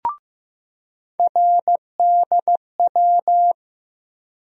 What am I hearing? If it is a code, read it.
Morse code, heard as a keyed tone, about 15 wpm, 700 Hz. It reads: RDW
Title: USA Call Sign Postfixes - 3 Letters 15wpm